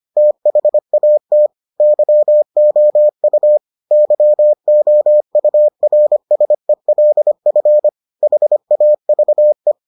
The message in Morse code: THAT YOU YOURSELF HAVE

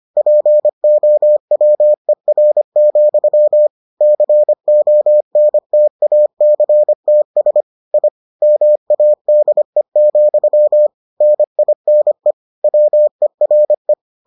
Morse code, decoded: POWER, CONTACTS I MADE, NINE WERE

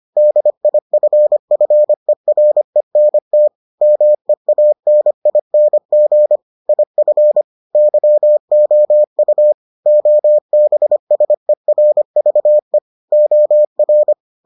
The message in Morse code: DIFFERENT MEANING IF YOU OBSERVE OR

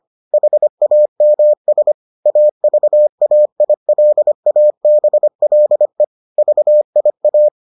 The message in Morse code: HAMS AVAILABLE VIA

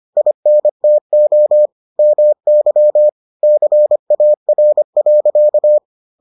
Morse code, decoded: INTO MY CAR.

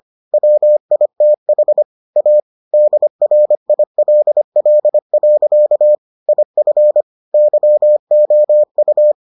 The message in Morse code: WITH A DRILL. IF YOU

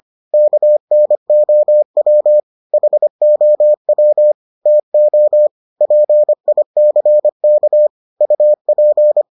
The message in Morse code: KNOW HOW TO PICK UP

